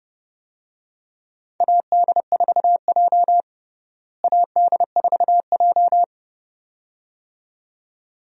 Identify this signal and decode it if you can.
Morse code, heard as a keyed tone, about 30 wpm, 700 Hz. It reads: AD4J AD4J
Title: USA Call Signs - 4 Characters - Repeated 2x - 30wpm